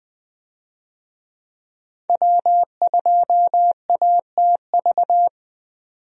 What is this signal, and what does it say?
Morse code, heard as a keyed tone, about 20 wpm, 700 Hz. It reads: W2ATV
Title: USA Call Signs - 5 Characters - 20wpm